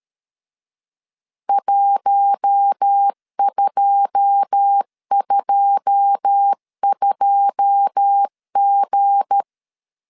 Morse code, heard as a keyed tone, about 13 wpm, 800 Hz. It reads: 1222G
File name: LEZIONI / 9 2 12wpm 2301